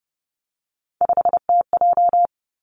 Morse code, heard as a keyed tone, about 30 wpm, 700 Hz. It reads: 5TJ
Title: Three Challenging Letter-Number Combinations 30wpm